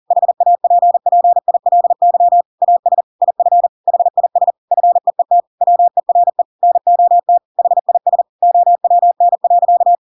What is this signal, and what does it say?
Morse code, heard as a keyed tone, about 40 wpm, 700 Hz. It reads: HAPPILY AS IF HIS FEET WERE NOT HIS OWN.
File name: Pinocchio / Pinocchio ch6 40 40 ik1hge